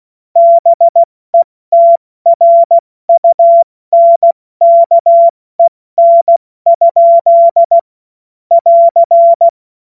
Morse code, heard as a keyed tone, about 16 wpm, spaced as for 14 wpm, 680 Hz. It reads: BETRUNKEN? <AR>
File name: Morsekurs / Witze / Witz 14. März 2023 70 F80